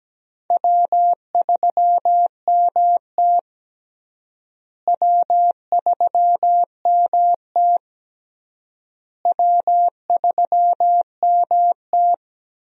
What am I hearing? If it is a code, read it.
Morse code, heard as a keyed tone, about 17 wpm, 700 Hz. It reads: W3MT W3MT W3MT
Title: USA Call Signs - 4 Characters - Repeated 3x - 17wpm